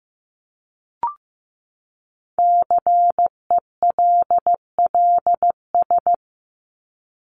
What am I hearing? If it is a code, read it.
Morse code, heard as a keyed tone, about 15 wpm, 700 Hz. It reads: CELLS
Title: ES:S Suffix - Top 100 Words with Base Words 15wpm